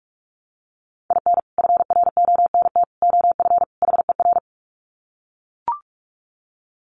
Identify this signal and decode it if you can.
Morse code, heard as a keyed tone, about 44 wpm, 700 Hz. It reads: IN FRONT OF HER
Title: Sets of 4 Words 45wpm